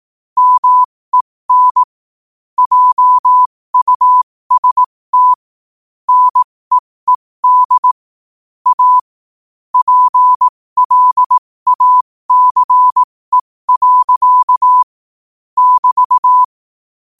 Morse code, heard as a keyed tone, about 18 wpm, spaced as for 15 wpm, 1000 Hz. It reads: MEN JUST NEED A PLACE. <BT>
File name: 15wpm